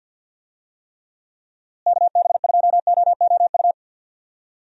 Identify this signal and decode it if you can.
Morse code, heard as a keyed tone, about 52 wpm, 700 Hz. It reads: KB2OOU
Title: USA Call Signs - 6 Characters - Repeated 2x - 50wpm